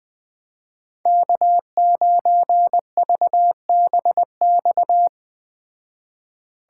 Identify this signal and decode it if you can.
Morse code, heard as a keyed tone, about 20 wpm, 700 Hz. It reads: K9VBX